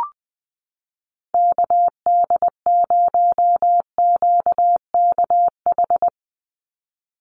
Morse code, heard as a keyed tone, about 20 wpm, 700 Hz. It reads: KD0QKH
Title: USA Call Signs 20wpm